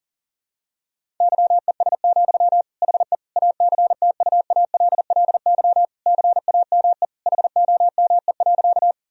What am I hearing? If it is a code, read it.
Morse code, heard as a keyed tone, about 40 wpm, 700 Hz. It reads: YES, HE ACTUALLY CAME HOME.
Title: Sentences from Top 1000 Words - Review Part 4 - 40wpm